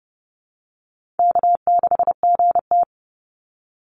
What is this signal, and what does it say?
Morse code, heard as a keyed tone, about 30 wpm, 700 Hz. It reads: K6GT
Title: Call Signs - 30wpm